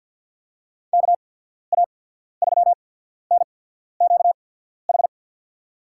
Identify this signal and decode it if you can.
Morse code, heard as a keyed tone, about 48 wpm, spaced as for 14 wpm, 700 Hz. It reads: KA3NQH
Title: USA Call Signs - 6 Characters - 8x Character Spacing - 50wpm